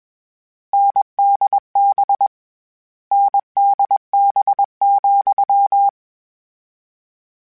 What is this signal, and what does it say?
Morse code, heard as a keyed tone, about 21 wpm, 800 Hz. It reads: NDB NDB,
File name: Difficult Combos / Difficult Combos-mp3s / NDB NDB6 @20